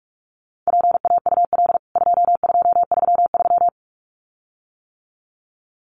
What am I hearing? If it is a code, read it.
Morse code, heard as a keyed tone, about 44 wpm, 700 Hz. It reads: PAUL 2233
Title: CWT Exchange 45wpm